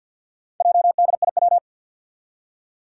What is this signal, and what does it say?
Morse code, heard as a keyed tone, about 51 wpm, 700 Hz. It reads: JDIW
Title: Four Challenging Letter-Number Combinations 50wpm